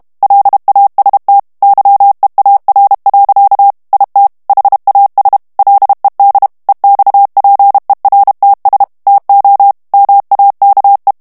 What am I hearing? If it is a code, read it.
Morse code, heard as a keyed tone, about 32 wpm, 800 Hz. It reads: LAST YEAR. IT HAS LED EXPERTS TO MAKE
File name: ANGLAIS texte clair 2 vitesse 30 mots minute